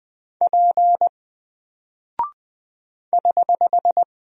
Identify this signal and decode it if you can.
Morse code, heard as a keyed tone, about 20 wpm, 700 Hz. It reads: P <HH>
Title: Single Character (Letter:Number:Punctuation:Prosign) 20wpm